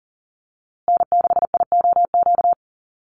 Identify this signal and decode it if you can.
Morse code, heard as a keyed tone, about 40 wpm, 700 Hz. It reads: N6IOQ
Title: USA Call Signs Repeated 3x 40wpm